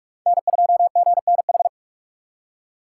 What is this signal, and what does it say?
Morse code, heard as a keyed tone, about 38 wpm, 700 Hz. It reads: N2GNH